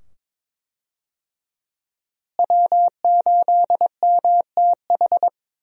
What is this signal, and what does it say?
Morse code, heard as a keyed tone, about 22 wpm, 700 Hz. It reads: W8MTH
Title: USA Call Signs - 5 Characters - Repeated 2x - 22wpm